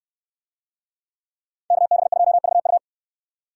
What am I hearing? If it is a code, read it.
Morse code, heard as a keyed tone, about 69 wpm, 700 Hz. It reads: KB2VU